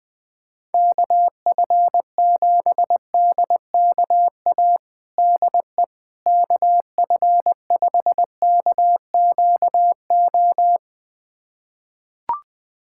Morse code, heard as a keyed tone, about 21 wpm, 700 Hz. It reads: KF7DKA DE KF5KQO
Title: USA Callsign DE Callsign - 6 Characters - 20wpm